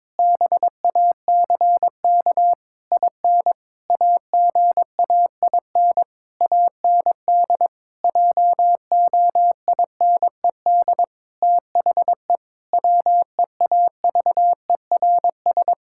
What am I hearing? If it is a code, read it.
Morse code, heard as a keyed tone, about 22 wpm, 700 Hz. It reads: BACK IN AGAIN AND JOINED THE WEAVERS
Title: Pangram Challenge v2 - 22wpm